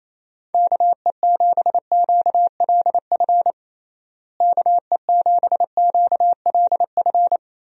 Morse code, heard as a keyed tone, about 27 wpm, 700 Hz. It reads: KE7QLF KE7QLF
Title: USA Call Signs - 6 Characters - Repeated 2x - 28wpm